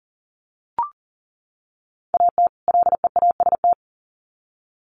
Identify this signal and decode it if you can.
Morse code, heard as a keyed tone, about 40 wpm, 700 Hz. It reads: AT LEAST
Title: Sets of 2 Words 40wpm